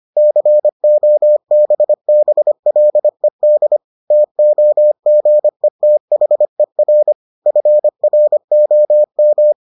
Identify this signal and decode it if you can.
Morse code, heard as a keyed tone, about 25 wpm, 600 Hz. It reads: COBBLED TOGETHER FROM